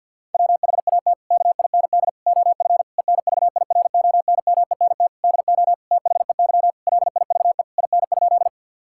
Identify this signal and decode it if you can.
Morse code, heard as a keyed tone, about 50 wpm, 700 Hz. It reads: WHAT KIND OF ENVIRONMENT DO THEY LIVE IN?
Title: Sentences from Top 1000 Words - Review Part 5 - 50wpm